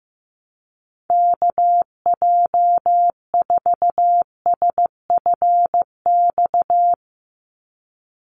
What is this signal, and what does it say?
Morse code, heard as a keyed tone, about 15 wpm, 700 Hz. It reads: KJ4SFX
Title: USA Call Signs 15wpm